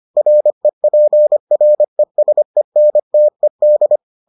Morse code, heard as a keyed tone, about 25 wpm, 600 Hz. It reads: REPRESENTED